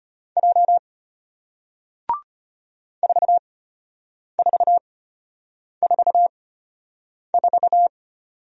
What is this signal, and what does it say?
Morse code, heard as a keyed tone, about 33 wpm, spaced as for 8 wpm, 700 Hz. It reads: J 4444